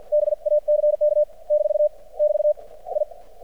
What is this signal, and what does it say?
Morse code, heard as a keyed tone, about 53 wpm, spaced as for 46 wpm, 600 Hz. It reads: BACK - <BT> I
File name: archived / CRSnet